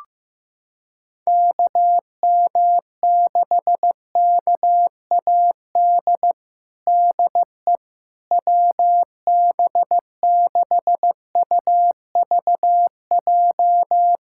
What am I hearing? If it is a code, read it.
Morse code, heard as a keyed tone, about 15 wpm, 700 Hz. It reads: KM6KAD DE WB6UVJ
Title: USA Callsign DE Callsign - 6 Characters - 15wpm